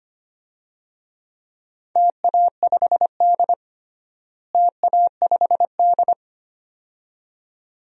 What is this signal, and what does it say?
Morse code, heard as a keyed tone, about 25 wpm, 700 Hz. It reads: TA5D TA5D
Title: DXCC-100 Call Signs - 4 Characters - Repeated 2x - 25wpm